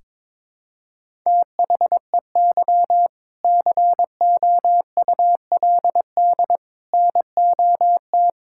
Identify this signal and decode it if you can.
Morse code, heard as a keyed tone, about 21 wpm, 700 Hz. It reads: THEY COULD NOT
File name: Sentences with 7-Word Phrases - 22wpm